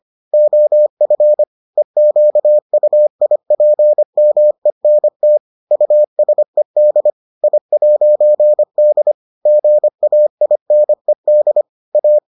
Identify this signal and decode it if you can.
Morse code, heard as a keyed tone, about 25 wpm, 600 Hz. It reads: OF EQUIPMENT USED I'D GAINED A